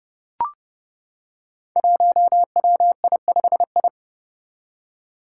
Morse code, heard as a keyed tone, about 30 wpm, 700 Hz. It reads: 1WI5I
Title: Five Challenging Letter-Number Combinations 30wpm